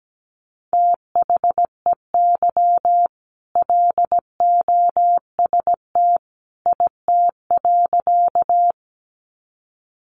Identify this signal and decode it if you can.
Morse code, heard as a keyed tone, about 17 wpm, 700 Hz. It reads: THEY LOST IT.
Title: Sentences from Top 1400 Words 17wpm